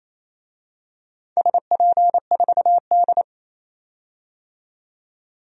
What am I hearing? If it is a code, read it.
Morse code, heard as a keyed tone, about 27 wpm, 690 Hz. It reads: SP4D
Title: DXCC-100 Call Signs - 4 Characters - 28wpm